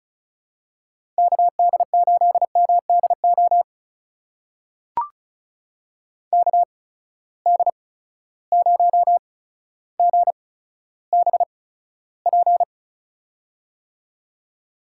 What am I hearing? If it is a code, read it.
Morse code, heard as a keyed tone, about 36 wpm, spaced as for 15 wpm, 700 Hz. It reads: KD8MDO KD0GBP
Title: USA Call Signs - 6 Characters - 8x Character Spacing - 35wpm